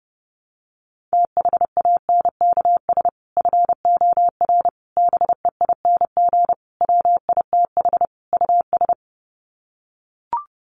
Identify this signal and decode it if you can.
Morse code, heard as a keyed tone, about 30 wpm, 700 Hz. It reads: THANKS FOR BEING WITH US
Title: Sets of 5 Words 30wpm